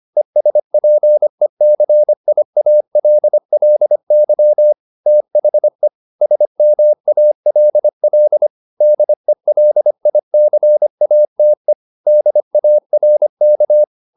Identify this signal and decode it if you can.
Morse code, heard as a keyed tone, about 25 wpm, 600 Hz. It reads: ESPECIALLY THE SMALL DELICATE DARK
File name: the old man and the sea part 1 25wpm